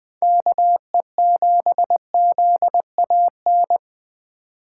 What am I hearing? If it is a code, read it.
Morse code, heard as a keyed tone, about 20 wpm, 700 Hz. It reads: KE7ZAN
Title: USA Call Signs - 6 Characters - 8x Character Spacing - 20wpm